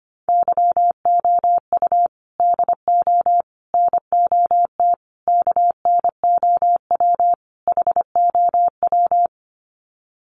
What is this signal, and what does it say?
Morse code, heard as a keyed tone, about 25 wpm, 700 Hz. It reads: YOU DO NOT KNOW HOW
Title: Sets of 5 Words Encore 25wpm